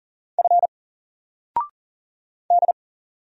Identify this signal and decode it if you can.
Morse code, heard as a keyed tone, about 40 wpm, 700 Hz. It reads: F D
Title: Single Character (Letter:Number:Punctuation:Prosign) 40wpm